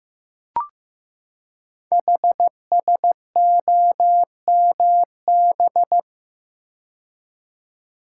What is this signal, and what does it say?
Morse code, heard as a keyed tone, about 15 wpm, 700 Hz. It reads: HSOMB
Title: Code Groups - 5 Letters - 15wpm